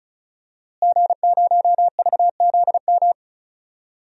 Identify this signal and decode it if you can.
Morse code, heard as a keyed tone, about 34 wpm, 700 Hz. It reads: G0VZM
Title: DXCC-100 Call Signs - 5 Characters - Repeated 3x - 35wpm